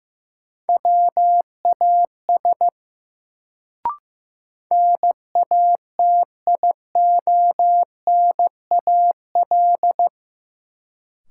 Morse code, heard as a keyed tone, about 15 wpm, 700 Hz. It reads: WAS NATIONAL
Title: Sets of 2 Words - New Word Drill - 15wpm